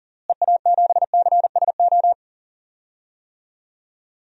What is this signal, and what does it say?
Morse code, heard as a keyed tone, about 40 wpm, 700 Hz. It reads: EA7CSO
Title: DXCC-100 Call Signs - 6 Characters - Repeated 3x - 40wpm